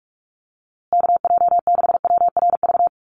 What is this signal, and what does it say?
Morse code, heard as a keyed tone, about 44 wpm, 700 Hz. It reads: KJ6WRV
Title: USA Call Signs Repeated 2x 45wpm